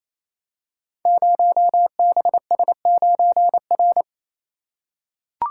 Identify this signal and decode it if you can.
Morse code, heard as a keyed tone, about 29 wpm, 700 Hz. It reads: 0BS9R
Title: Code Groups - 5 Letter:Numbers - 28wpm